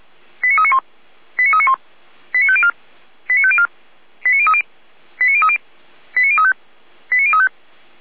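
Five-tone selective call (ZVEI), heard as quick runs of five tones, tones of about 70 ms. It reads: 89391 89391 89504 89504 89030 89030 89035 89035
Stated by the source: ZVEI five-tone selective call 89042 received by SDR.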